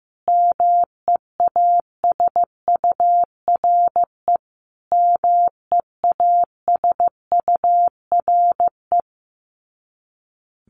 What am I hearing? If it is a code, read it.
Morse code, heard as a keyed tone, about 15 wpm, 700 Hz. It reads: MEASURE MEASURE
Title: Top 500 Words Repeated 15wpm